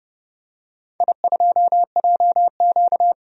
Morse code, heard as a keyed tone, about 30 wpm, 700 Hz. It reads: I2JQ
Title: DXCC-100 Call Signs - 4 Characters - 30wpm